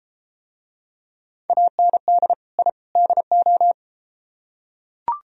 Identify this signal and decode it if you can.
Morse code, heard as a keyed tone, about 33 wpm, 700 Hz. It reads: AND I DO